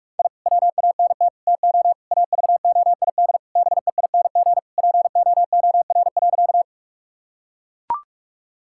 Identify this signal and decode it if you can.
Morse code, heard as a keyed tone, about 43 wpm, 700 Hz. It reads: I WANT TO AVOID BEING POOR.
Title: Sentences from Top 1000 Words - Review Part 6 - 45wpm